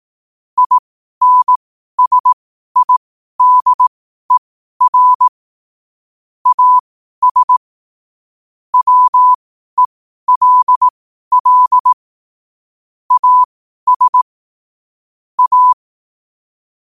Morse code, heard as a keyed tone, about 18 wpm, spaced as for 12 wpm, 1000 Hz. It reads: INSIDER AS WELL AS A
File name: qotdfiles / 12wpm